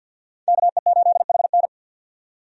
Morse code, heard as a keyed tone, about 46 wpm, 700 Hz. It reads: KE8HN
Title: USA Call Signs - 5 Characters - 50wpm